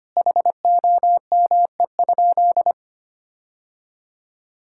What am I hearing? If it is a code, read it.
Morse code, heard as a keyed tone, about 25 wpm, 700 Hz. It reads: HOME?
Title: Sentences from Top 1600 Words 25wpm